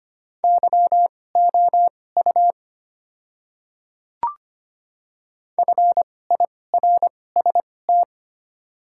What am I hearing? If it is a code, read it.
Morse code, heard as a keyed tone, about 25 wpm, spaced as for 18 wpm, 700 Hz. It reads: YOU FIRST
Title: 2x Character Spacing - Top 100 Words - 25wpm